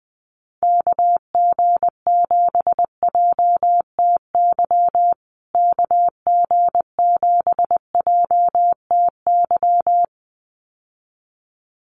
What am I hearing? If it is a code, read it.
Morse code, heard as a keyed tone, about 20 wpm, 700 Hz. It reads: KG7JTY KG7JTY
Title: USA Call Signs Repeated 2x 20wpm